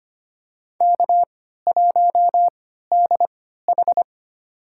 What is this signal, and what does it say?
Morse code, heard as a keyed tone, about 25 wpm, spaced as for 14 wpm, 700 Hz. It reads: K1DH
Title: USA Call Signs - 3x Character Spacing - 25wpm